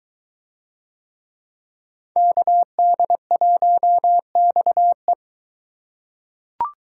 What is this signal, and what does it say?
Morse code, heard as a keyed tone, about 23 wpm, 700 Hz. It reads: KD1XE